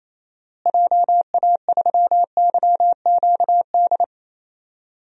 Morse code, heard as a keyed tone, about 27 wpm, 700 Hz. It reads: JA3YQD
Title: DXCC-100 Call Signs - 6 Characters - Repeated 3x - 28wpm